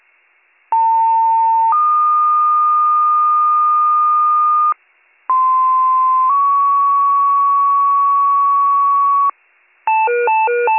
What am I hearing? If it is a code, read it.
Two-tone page, heard as a one-second tone, then a three-second tone, about 900 then 1220 Hz, about 1040 then 1120 Hz. It is heard from 0.7 to 9.3 s.